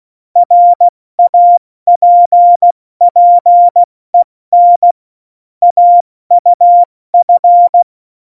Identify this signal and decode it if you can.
Morse code, heard as a keyed tone, about 16 wpm, 700 Hz. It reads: RAPPEN AUF
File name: Morsekurs / Witze / Witz 15. Oktober 2019, 70 F80